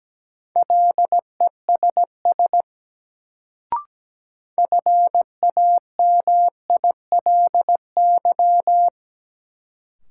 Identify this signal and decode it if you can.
Morse code, heard as a keyed tone, about 17 wpm, 700 Hz. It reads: LESS FAMILY
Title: Sentences from Top 400 Words - New Word Drill - 17wpm